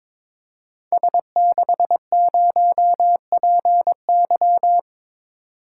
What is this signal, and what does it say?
Morse code, heard as a keyed tone, about 22 wpm, 700 Hz. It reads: S60PY
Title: Code Groups - 5 Letter:Numbers - 22wpm